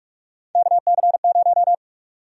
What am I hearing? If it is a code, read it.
Morse code, heard as a keyed tone, about 46 wpm, 700 Hz. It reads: KC0
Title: USA Call Sign Prefixes - 45wpm